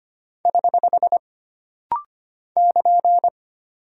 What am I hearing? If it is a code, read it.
Morse code, heard as a keyed tone, about 25 wpm, 700 Hz. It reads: <HH> <KN>
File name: Single Punctuation-Prosign - Rapid-Fire - 25wpm